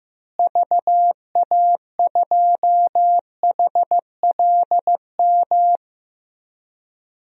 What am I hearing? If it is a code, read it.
Morse code, heard as a keyed tone, about 14 wpm, 700 Hz. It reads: VA2HLM
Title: DXCC-100 Call Signs - 6 Characters - Repeated 3x - 15wpm